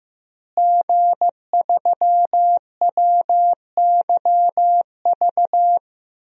Morse code, heard as a keyed tone, about 15 wpm, 700 Hz. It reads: G3WYV
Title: DXCC-100 Call Signs - 5 Characters - Repeated 3x - 15wpm